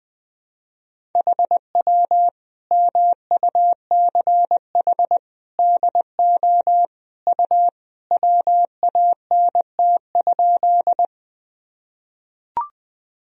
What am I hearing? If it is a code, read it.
Morse code, heard as a keyed tone, about 21 wpm, 700 Hz. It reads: HW MUCH DO U WANT?
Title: Sentences from Top 200 Words and Abbreviations 20wpm